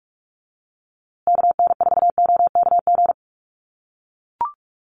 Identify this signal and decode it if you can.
Morse code, heard as a keyed tone, about 44 wpm, 700 Hz. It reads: KN4OKG
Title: USA Call Signs Repeated 3x 45wpm